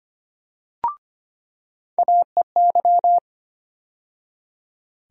Morse code, heard as a keyed tone, about 25 wpm, 700 Hz. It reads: AEY
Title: USA Call Sign Postfixes - 3 Letters 25wpm